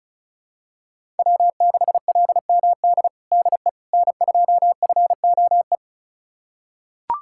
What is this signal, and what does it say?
Morse code, heard as a keyed tone, about 36 wpm, 700 Hz. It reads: W6LMD DE N2FOE
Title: USA Callsign DE Callsign - 5 Characters - 35wpm